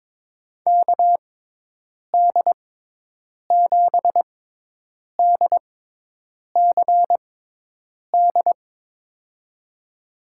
Morse code, heard as a keyed tone, about 21 wpm, spaced as for 8 wpm, 700 Hz. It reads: KD7DCD
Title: USA Call Signs - 6 Characters - 6x Character Spacing - 22wpm